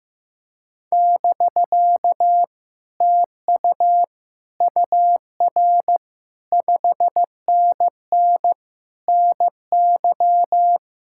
Morse code, heard as a keyed tone, about 15 wpm, 700 Hz. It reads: <BK> TU UR 5NN NY
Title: Speed-Racing Common QSO Phrases Encore - 15wpm